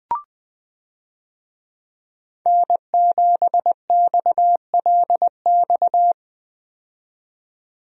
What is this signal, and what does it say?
Morse code, heard as a keyed tone, about 20 wpm, 700 Hz. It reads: N7XLX
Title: USA Call Signs - 5 Characters - 20wpm